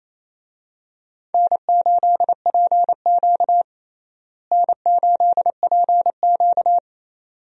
Morse code, heard as a keyed tone, about 27 wpm, 700 Hz. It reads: N8PQ N8PQ
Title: USA Call Signs - 4 Characters - Repeated 2x - 28wpm